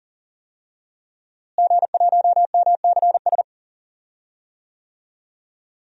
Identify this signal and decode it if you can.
Morse code, heard as a keyed tone, about 37 wpm, 700 Hz. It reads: G1MCS
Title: DXCC-100 Call Signs - 5 Characters - Repeated 3x - 40wpm